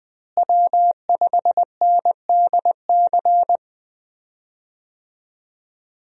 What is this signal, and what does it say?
Morse code, heard as a keyed tone, about 20 wpm, 700 Hz. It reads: W5NDC
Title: USA Call Signs - 5 Characters - Repeated 3x - 20wpm